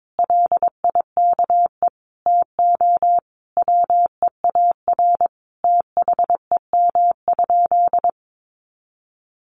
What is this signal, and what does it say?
Morse code, heard as a keyed tone, about 22 wpm, 700 Hz. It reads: LIKE TO WEAR THEM?
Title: Sentences from Top 1500 Words 22wpm